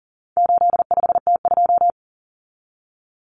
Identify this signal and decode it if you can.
Morse code, heard as a keyed tone, about 40 wpm, 700 Hz. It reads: 85T2
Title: Common Cut Numbers - 4-Digit 40wpm